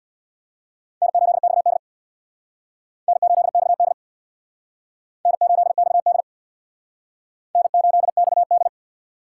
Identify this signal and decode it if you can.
Morse code, heard as a keyed tone, about 61 wpm, 700 Hz. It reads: N8XD N8XD N8XD N8XD
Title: Speed Racing - USA Call Signs 50wpm